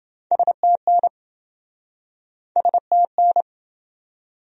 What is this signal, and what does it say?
Morse code, heard as a keyed tone, about 28 wpm, 700 Hz. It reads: STN STN